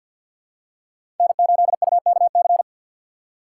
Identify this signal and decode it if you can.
Morse code, heard as a keyed tone, about 52 wpm, 700 Hz. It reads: N8UKC
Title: USA Call Signs - 5 Characters - 4x Character Spacing - 50wpm